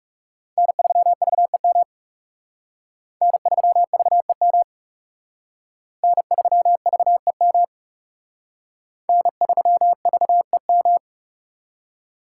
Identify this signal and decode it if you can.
Morse code, heard as a keyed tone, about 35 wpm, 700 Hz. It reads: N3VEM N3VEM N3VEM N3VEM
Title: Speed Racing - USA 1x3 Call Signs 30wpm